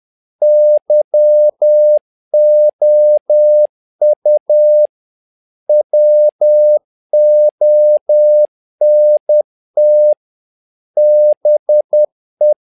Morse code, heard as a keyed tone, about 10 wpm, 600 Hz. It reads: YOU WONT BE
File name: learning center / tower trouble / tower trouble 10wpm